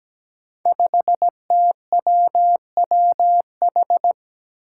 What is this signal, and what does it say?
Morse code, heard as a keyed tone, about 17 wpm, 700 Hz. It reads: 5TWWH
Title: Five Challenging Letter-Number Combinations 17wpm